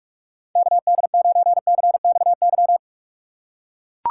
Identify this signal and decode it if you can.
Morse code, heard as a keyed tone, about 43 wpm, 700 Hz. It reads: KD9CXY
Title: USA Call Signs - 6 Characters - 3x Character Spacing - 45wpm